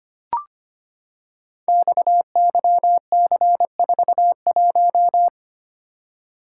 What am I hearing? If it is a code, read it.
Morse code, heard as a keyed tone, about 25 wpm, 700 Hz. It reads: XYC41
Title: Code Groups - 5 Letter:Numbers - 25wpm